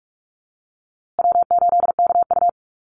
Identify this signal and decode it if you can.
Morse code, heard as a keyed tone, about 44 wpm, 700 Hz. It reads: W8KU
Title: USA Call Signs Repeated 2x 45wpm